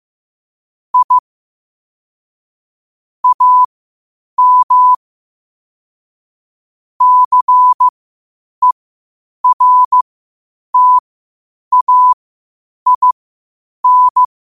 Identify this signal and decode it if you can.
Morse code, heard as a keyed tone, about 15 wpm, spaced as for 8 wpm, 1000 Hz. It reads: I AM CERTAIN
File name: qotdfiles / 8wpm